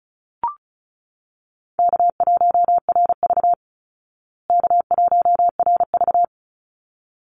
Morse code, heard as a keyed tone, about 34 wpm, 700 Hz. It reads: K1RV K1RV
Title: Call Signs Repeated 3x - 35wpm